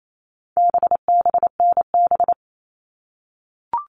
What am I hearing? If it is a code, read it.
Morse code, heard as a keyed tone, about 28 wpm, 700 Hz. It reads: BBNB
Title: All Cut Numbers - 4-Digit 28wpm